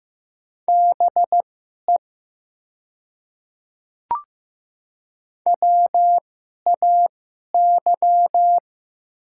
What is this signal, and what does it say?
Morse code, heard as a keyed tone, about 15 wpm, spaced as for 11 wpm, 700 Hz. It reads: BE WAY
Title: 2x Character Spacing - Top 100 Words - 15wpm